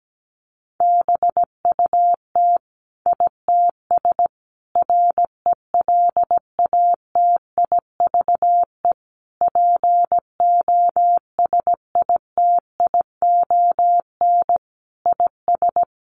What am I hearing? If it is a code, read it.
Morse code, heard as a keyed tone, about 17 wpm, 700 Hz. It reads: BUT ITS RELATIVE POSITION IS
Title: Sentences from Top 1400 Words 17wpm